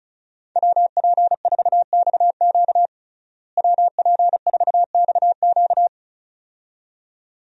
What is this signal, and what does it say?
Morse code, heard as a keyed tone, about 36 wpm, 700 Hz. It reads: WP4XQ WP4XQ
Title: USA Call Signs - 5 Characters - Repeated 2x - 35wpm